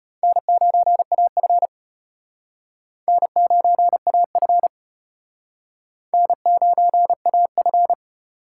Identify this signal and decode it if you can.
Morse code, heard as a keyed tone, about 31 wpm, 700 Hz. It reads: N9AF N9AF N9AF